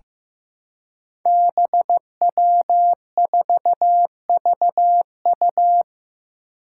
Morse code, heard as a keyed tone, about 15 wpm, 700 Hz. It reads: BW4VU
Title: Five Challenging Letter-Number Combinations 15wpm